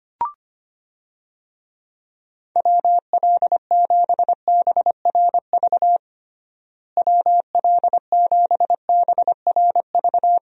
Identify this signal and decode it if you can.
Morse code, heard as a keyed tone, about 25 wpm, 700 Hz. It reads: WL7BRV WL7BRV
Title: USA Call Signs - 6 Characters - Repeated 2x - 25wpm